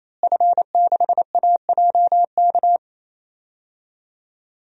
Morse code, heard as a keyed tone, about 28 wpm, 700 Hz. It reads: F6AJK
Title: DXCC-100 Call Signs - 5 Characters - Repeated 3x - 28wpm